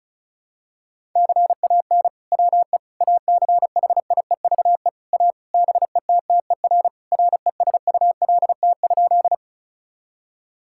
Morse code, heard as a keyed tone, about 35 wpm, 700 Hz. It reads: CAN WE ACHIEVE A BETTER RESULT?
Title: Sentences from Top 1000 Words - Review Part 6 - 35wpm